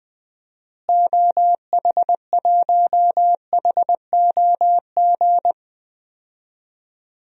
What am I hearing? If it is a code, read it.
Morse code, heard as a keyed tone, about 21 wpm, 700 Hz. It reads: OH1HOG
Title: DXCC-100 Call Signs - 6 Characters - Repeated 2x - 20wpm